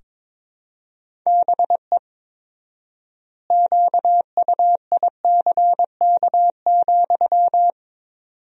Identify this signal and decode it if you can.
Morse code, heard as a keyed tone, about 22 wpm, 700 Hz. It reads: BE QUICK,
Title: 4x Word-Spacing - Sentences from Top 500 Words - 22wpm